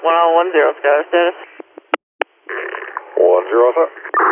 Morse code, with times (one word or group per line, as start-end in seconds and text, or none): none